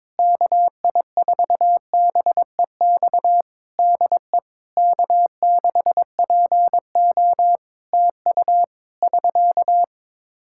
Morse code, heard as a keyed tone, about 21 wpm, 700 Hz. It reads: KI4BEX DE K6PO TU <SK>
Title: Practice QSO - Exchange 4 - Ending - 22wpm